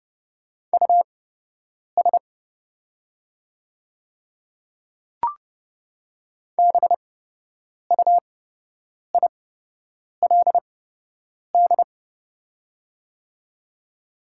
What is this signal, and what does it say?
Morse code, heard as a keyed tone, about 30 wpm, spaced as for 8 wpm, 700 Hz. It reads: US BUILD
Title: 8x Character Spacing - Top 200 Words - 30wpm